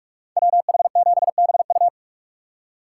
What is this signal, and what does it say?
Morse code, heard as a keyed tone, about 54 wpm, 700 Hz. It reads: WH7BU
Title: Speed Racing - USA Call Signs 45wpm